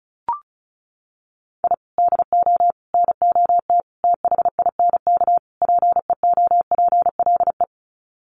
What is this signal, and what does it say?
Morse code, heard as a keyed tone, about 34 wpm, 700 Hz. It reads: I DO NOT THINK PEOPLE
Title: Sets of 5 Words Encore2 35wpm